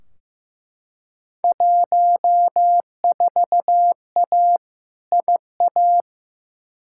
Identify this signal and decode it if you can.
Morse code, heard as a keyed tone, about 15 wpm, 700 Hz. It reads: 14A IA
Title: Field Day Exchanges Encore - 15wpm